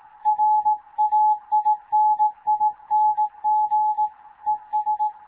Morse code, heard as a keyed tone, about 18 wpm, 850 Hz. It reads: RAINING ES
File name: Morze-HA2MN / hangp / 2 11 GYAKORLAT